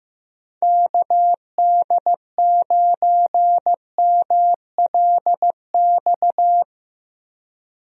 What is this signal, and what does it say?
Morse code, heard as a keyed tone, about 15 wpm, 700 Hz. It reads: KD9MLX
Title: USA Call Signs - 6 Characters - Repeated 3x - 15wpm